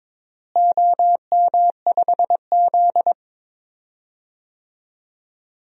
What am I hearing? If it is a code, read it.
Morse code, heard as a keyed tone, about 22 wpm, 700 Hz. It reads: OM5Z
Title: DXCC-100 Call Signs - 4 Characters - Repeated 3x - 22wpm